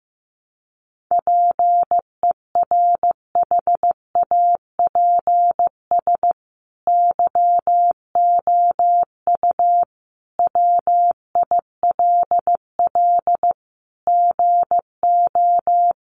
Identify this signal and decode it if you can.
Morse code, heard as a keyed tone, about 14 wpm, 700 Hz. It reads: PERHAPS YOU WILL GO
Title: Sentences from Top 500 Words 15wpm